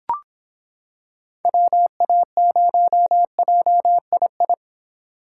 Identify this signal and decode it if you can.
Morse code, heard as a keyed tone, about 25 wpm, 700 Hz. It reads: WA0JII